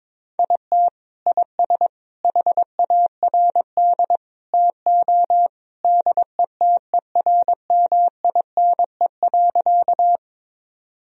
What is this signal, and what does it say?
Morse code, heard as a keyed tone, about 21 wpm, 700 Hz. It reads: IT IS HARD TO DETERMINE.
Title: Sentences from Top 1000 Words - Review Part 4 - 22wpm